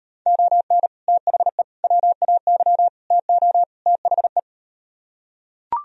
Morse code, heard as a keyed tone, about 38 wpm, 700 Hz. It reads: ON THE WAY TO THE